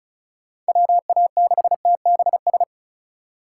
Morse code, heard as a keyed tone, about 36 wpm, 700 Hz. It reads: WA6TBS
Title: USA Call Signs - 6 Characters - 6x Character Spacing - 35wpm